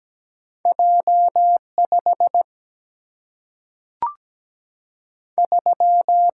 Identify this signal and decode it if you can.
Morse code, heard as a keyed tone, about 17 wpm, 700 Hz. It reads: J 5 3
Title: Single-Letter Single-Number 17wpm